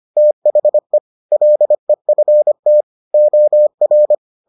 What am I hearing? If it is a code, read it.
Morse code, heard as a keyed tone, about 25 wpm, 600 Hz. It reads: THE LEFT OR